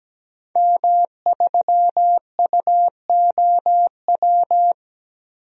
Morse code, heard as a keyed tone, about 17 wpm, 700 Hz. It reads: M3UOW
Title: DXCC-100 Call Signs - 5 Characters - 17wpm